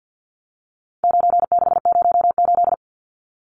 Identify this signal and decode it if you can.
Morse code, heard as a keyed tone, about 50 wpm, 700 Hz. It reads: 9608
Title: Numbers - 4-Digit 50wpm